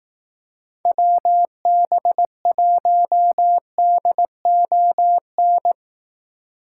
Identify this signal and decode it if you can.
Morse code, heard as a keyed tone, about 18 wpm, 700 Hz. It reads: WB1DON
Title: Speed Racing - USA 2x3 Call Signs 15wpm